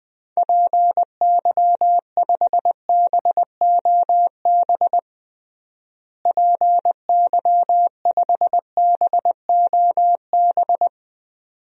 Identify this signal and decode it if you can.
Morse code, heard as a keyed tone, about 20 wpm, 700 Hz. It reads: PY5BOB PY5BOB
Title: DXCC-100 Call Signs - 6 Characters - Repeated 2x - 20wpm